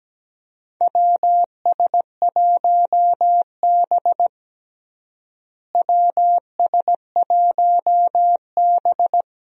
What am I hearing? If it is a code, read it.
Morse code, heard as a keyed tone, about 17 wpm, 700 Hz. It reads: WS1B WS1B
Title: USA Call Signs - 4 Characters - Repeated 2x - 17wpm